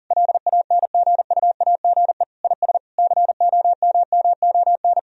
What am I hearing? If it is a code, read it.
Morse code, heard as a keyed tone, about 40 wpm, 700 Hz. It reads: LANGUAGE IS COMMON
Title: Sentences from Top 1000 Words - Review Part 3 - 40wpm